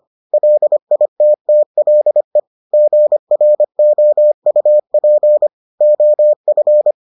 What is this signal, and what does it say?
Morse code, heard as a keyed tone, about 25 wpm, 600 Hz. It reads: LITTLE GROUP OF